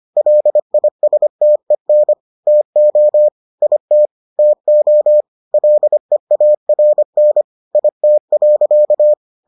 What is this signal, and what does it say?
Morse code, heard as a keyed tone, about 25 wpm, 600 Hz. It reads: LISTEN TO IT TO LEARN IT.